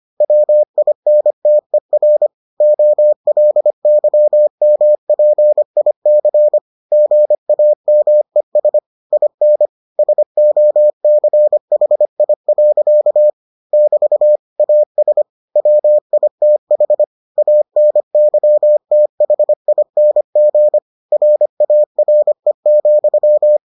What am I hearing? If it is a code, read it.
Morse code, heard as a keyed tone, about 25 wpm, 600 Hz. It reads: WINTER OLYMPIC GAMES IN SOCHI. <BT> AS WITH ANYTHING RARE,